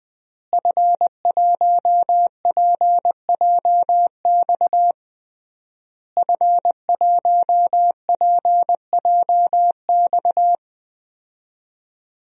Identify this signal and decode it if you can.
Morse code, heard as a keyed tone, about 20 wpm, 700 Hz. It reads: F1PJX F1PJX
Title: DXCC-100 Call Signs - 5 Characters - Repeated 3x - 20wpm